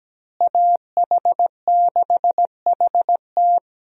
Morse code, heard as a keyed tone, about 17 wpm, 700 Hz. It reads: AH6HT
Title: USA Call Signs - 5 Characters - Repeated 3x - 17wpm